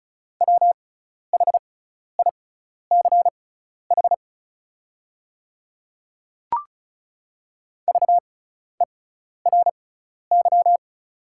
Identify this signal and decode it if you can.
Morse code, heard as a keyed tone, about 34 wpm, spaced as for 12 wpm, 690 Hz. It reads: WHICH VERY
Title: 6x Character Spacing - Top 200 Words - 35wpm